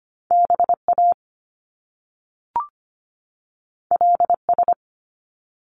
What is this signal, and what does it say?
Morse code, heard as a keyed tone, about 25 wpm, 700 Hz. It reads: BA LS
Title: Top Two Letter Combinations in Words 25wpm